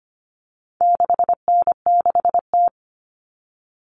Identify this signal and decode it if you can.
Morse code, heard as a keyed tone, about 25 wpm, 700 Hz. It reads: 6N6T
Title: All Cut Numbers - 4-Digit 25wpm